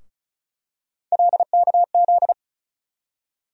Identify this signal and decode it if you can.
Morse code, heard as a keyed tone, about 36 wpm, 700 Hz. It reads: LKZ
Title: USA Call Sign Postfixes - 3 Letters 35wpm